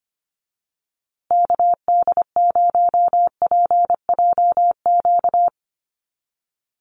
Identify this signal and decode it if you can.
Morse code, heard as a keyed tone, about 25 wpm, 700 Hz. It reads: KD0PJQ
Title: USA Call Signs Repeated 2x 25wpm